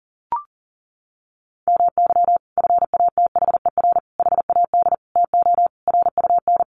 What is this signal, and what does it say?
Morse code, heard as a keyed tone, about 40 wpm, 700 Hz. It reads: MY FATHER HAD TO RUN
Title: Sentences from Top 1100 Words 40wpm